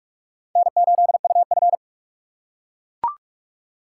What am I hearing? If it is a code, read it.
Morse code, heard as a keyed tone, about 43 wpm, 700 Hz. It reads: N8UF
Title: USA Call Signs - 4 Characters - 4x Character Spacing - 45wpm